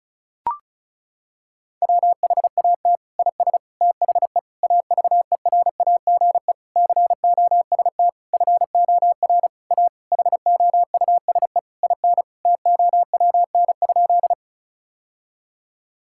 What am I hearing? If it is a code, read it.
Morse code, heard as a keyed tone, about 36 wpm, 700 Hz. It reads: WHAT IS THE AVERAGE COST FOR A HOUSE IN TOWN?
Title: Sentences from Top 1000 Words - Review Part 6 - 35wpm